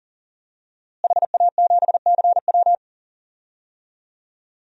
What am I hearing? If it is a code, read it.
Morse code, heard as a keyed tone, about 40 wpm, 700 Hz. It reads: HA7CW
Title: DXCC-100 Call Signs - 5 Characters - Repeated 3x - 40wpm